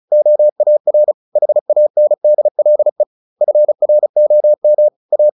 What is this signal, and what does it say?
Morse code, heard as a keyed tone, about 34 wpm, 600 Hz. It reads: OAR HANDLE FROM A
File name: the old man and the sea 2 35wpm